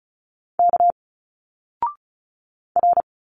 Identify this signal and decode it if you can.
Morse code, heard as a keyed tone, about 33 wpm, 700 Hz. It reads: K R
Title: Single Letters 35wpm